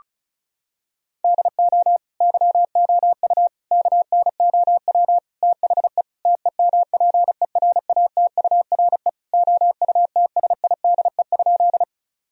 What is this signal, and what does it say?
Morse code, heard as a keyed tone, about 34 wpm, 700 Hz. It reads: DO YOU KNOW THE TEMPERATURE OUTSIDE?
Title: Sentences from Top 1000 Words - Review Part 4 - 35wpm